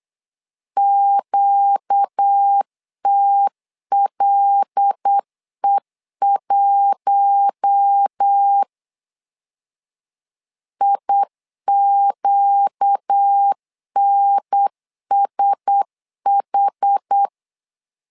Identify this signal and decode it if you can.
Morse code, heard as a keyed tone, about 8 wpm, 800 Hz. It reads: QTLE1 IQNSH
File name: LEZIONI / 5 LBVC / 5 L 8wpm 2831